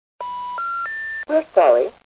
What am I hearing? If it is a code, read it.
Special information tone, with speech.